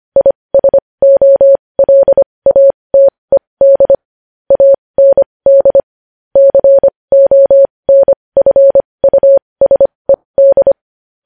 Morse code, heard as a keyed tone, about 25 wpm, spaced as for 20 wpm, 550 Hz. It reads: ISOLATED AND CONFUSED
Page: news-headlines-in-morse-code-at-20-wpm